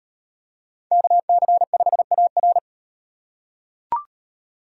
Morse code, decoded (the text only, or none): KC5AR